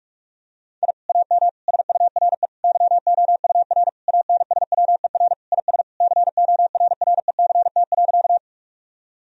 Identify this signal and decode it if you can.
Morse code, heard as a keyed tone, about 41 wpm, 700 Hz. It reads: I AM SURE YOUR ANSWER IS CORRECT.
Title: Sentences from Top 1000 Words - Review Part 2 - 45wpm